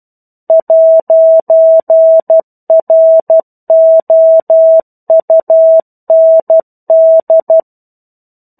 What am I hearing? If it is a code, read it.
Morse code, heard as a keyed tone, about 12 wpm, 650 Hz. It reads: 'ROUND
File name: 01-Chapter-01 12 WPM